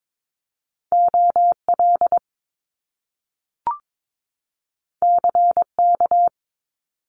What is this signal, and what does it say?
Morse code, heard as a keyed tone, about 22 wpm, 700 Hz. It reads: OL CK
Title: Top Two Letter Combinations in Words 22wpm